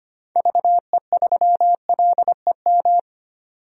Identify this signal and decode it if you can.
Morse code, heard as a keyed tone, about 25 wpm, 700 Hz. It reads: VE3LEM
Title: DXCC-100 Call Signs - 6 Characters - 25wpm